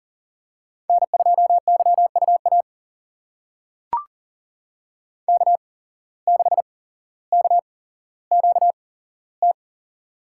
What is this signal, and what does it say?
Morse code, heard as a keyed tone, about 40 wpm, spaced as for 17 wpm, 700 Hz. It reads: N2YUA K6KQT
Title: USA Call Signs - 5 Characters - 8x Character Spacing - 40wpm